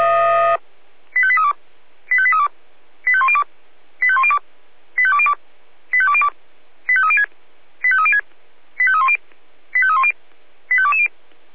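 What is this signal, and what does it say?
Five-tone selective call (ZVEI), heard as quick runs of five tones, tones of about 70 ms. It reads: 85832 85832 85102 85102 85202 85202 85306 85306 85310 85310 85220
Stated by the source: ZVEI five-tone selective call, received by SDR.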